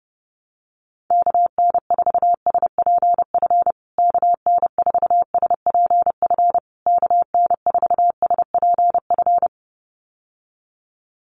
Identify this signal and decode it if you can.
Morse code, heard as a keyed tone, about 30 wpm, 700 Hz. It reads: KN4SPF KN4SPF KN4SPF
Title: USA Call Signs Repeated 3x 30wpm